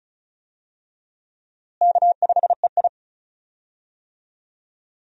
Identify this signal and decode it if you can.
Morse code, heard as a keyed tone, about 36 wpm, 700 Hz. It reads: K5EI
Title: USA Call Signs - 4 Characters - Repeated 3x - 35wpm